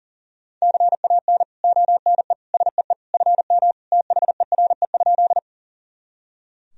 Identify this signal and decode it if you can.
Morse code, heard as a keyed tone, about 37 wpm, 700 Hz. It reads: CAN ONE SEE FM THERE?
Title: Sentences from Top 100 Words and Abbreviations 40wpm